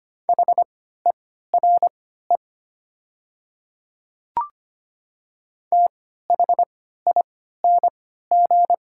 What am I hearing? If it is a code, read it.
Morse code, heard as a keyed tone, about 25 wpm, spaced as for 14 wpm, 700 Hz. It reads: HERE THING
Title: 3x Character Spacing - Top 300 Words - 25wpm